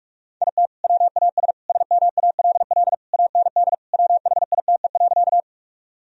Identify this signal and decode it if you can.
Morse code, heard as a keyed tone, about 46 wpm, 700 Hz. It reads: IT WAS SMALL AND WHITE.
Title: Sentences from Top 1000 Words - Review Part 1 - 45wpm